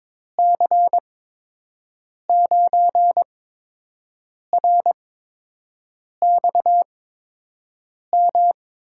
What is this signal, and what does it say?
Morse code, heard as a keyed tone, about 22 wpm, spaced as for 6 wpm, 700 Hz. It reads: C9RXM
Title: USA Call Signs - 6 Characters - 8x Character Spacing - 22wpm